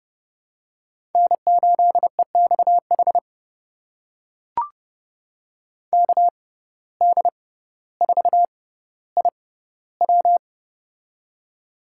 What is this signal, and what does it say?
Morse code, heard as a keyed tone, about 30 wpm, spaced as for 15 wpm, 700 Hz. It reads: N8EXH KD4IW
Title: USA Call Signs - 5 Characters - 6x Character Spacing - 30wpm